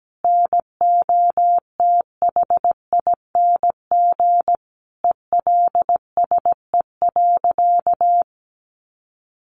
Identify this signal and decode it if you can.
Morse code, heard as a keyed tone, about 17 wpm, 700 Hz. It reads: NOTHING ELSE.
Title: Sentences from Top 1100 Words 17wpm